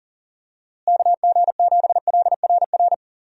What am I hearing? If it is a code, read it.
Morse code, heard as a keyed tone, about 40 wpm, 700 Hz. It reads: KG7LRR
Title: USA Call Signs - 6 Characters - Repeated 3x - 40wpm